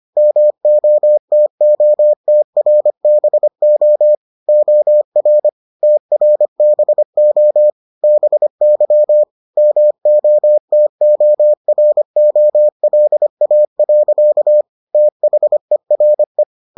Morse code, heard as a keyed tone, about 25 wpm, 600 Hz. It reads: MOTOTRBO OR TRBO BY MOTOROLA. THERE